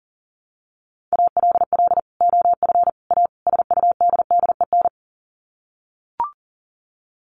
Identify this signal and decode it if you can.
Morse code, heard as a keyed tone, about 40 wpm, 700 Hz. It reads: ALL OF A SUDDEN
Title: Sets of 4 Words 40wpm